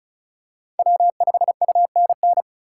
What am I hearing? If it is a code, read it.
Morse code, heard as a keyed tone, about 36 wpm, 690 Hz. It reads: W5UNN
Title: USA Call Signs - 5 Characters - 3x Character Spacing - 35wpm